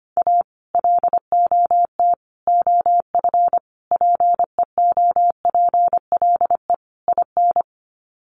Morse code, heard as a keyed tone, about 25 wpm, 700 Hz. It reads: A LOT OF PEOPLE IN
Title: Sets of 5 Words 25wpm